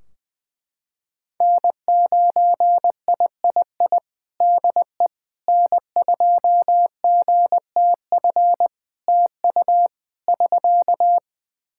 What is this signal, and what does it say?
Morse code, heard as a keyed tone, about 21 wpm, 700 Hz. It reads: N9III DE N2GTF TU <SK>
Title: Practice QSO - Exchange 4 - Ending - 20wpm